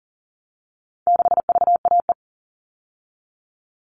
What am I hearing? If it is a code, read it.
Morse code, heard as a keyed tone, about 40 wpm, 700 Hz. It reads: 6VAE
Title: All Cut Numbers - 4-Digit 40wpm